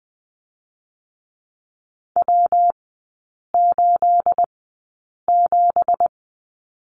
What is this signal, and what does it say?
Morse code, heard as a keyed tone, about 20 wpm, 700 Hz. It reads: W 8 7
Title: 218 Lesson 35e - Number 0 - Three Characters 20wpm